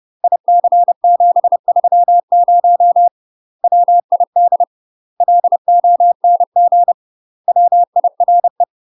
Morse code, heard as a keyed tone, about 30 wpm, 700 Hz. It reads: IC730 WID LONG WIRE
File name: UK MASTER QSOs TRAINING / 30WPM / UK MASTER QSOs 0016